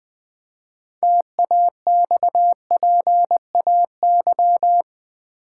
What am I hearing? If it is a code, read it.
Morse code, heard as a keyed tone, about 20 wpm, 700 Hz. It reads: TAXPAY
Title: Build A Word - 2 Letters At A Time - 8-Letter Words - 20wpm